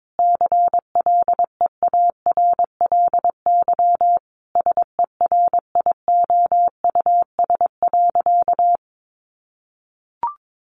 Morse code, heard as a keyed tone, about 22 wpm, 700 Hz. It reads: CLEARLY SERIOUS.
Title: Sentences from Top 1300 Words 22wpm